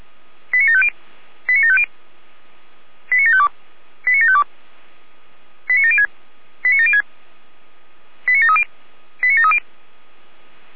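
Five-tone selective call (ZVEI), heard as quick runs of five tones, tones of about 80 ms. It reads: 89750 89750 89752 89752 89706 89706 89730 89730
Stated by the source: Five-tone ZVEI selective call received by SDR.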